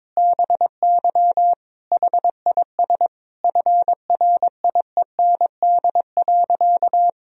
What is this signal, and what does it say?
Morse code, heard as a keyed tone, about 22 wpm, 700 Hz. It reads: BY HIS FRIEND.
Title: Sentences from Top 1000 Words - Review Part 4 - 22wpm